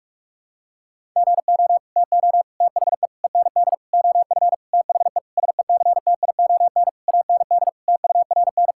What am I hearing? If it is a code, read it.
Morse code, heard as a keyed tone, about 46 wpm, 700 Hz. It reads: GO TO THE END OF THE SECTION AND TURN
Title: Sentences from Top 1000 Words - Review Part 5 - 45wpm